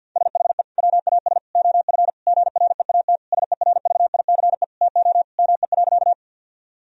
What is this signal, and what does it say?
Morse code, heard as a keyed tone, about 50 wpm, 700 Hz. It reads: SHE WAS OF GREAT SERVICE TO ME.
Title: Sentences from Top 1000 Words - Review Part 4 - 50wpm